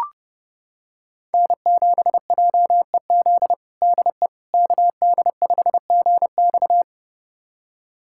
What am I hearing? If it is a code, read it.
Morse code, heard as a keyed tone, about 30 wpm, 700 Hz. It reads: N7JEZ DE KD5GX
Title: USA Callsign DE Callsign - 5 Characters - 30wpm